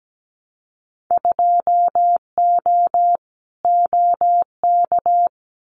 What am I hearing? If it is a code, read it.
Morse code, heard as a keyed tone, about 17 wpm, 700 Hz. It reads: 2O OK
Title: Winter Field Day Exchanges Encore 17wpm